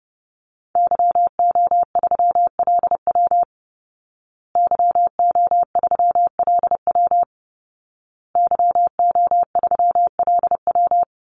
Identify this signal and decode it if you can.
Morse code, heard as a keyed tone, about 30 wpm, 700 Hz. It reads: YO3LW YO3LW YO3LW
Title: Call Signs Repeated 2x - 30wpm